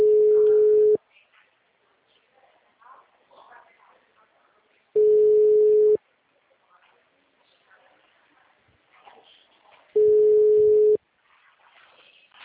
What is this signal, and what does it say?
Ringback tone (Europe, 425 Hz).
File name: out- -2007- - - .12713